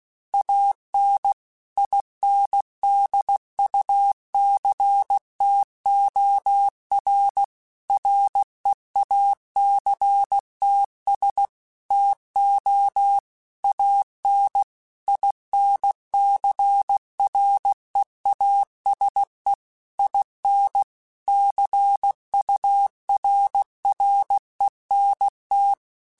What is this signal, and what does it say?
Morse code, heard as a keyed tone, about 16 wpm, 800 Hz. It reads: AN INDUCTOR REACTS TO AN INCREASE IN CURRENT
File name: Technical Subjects / Technical Subjects-mp3s / XL @15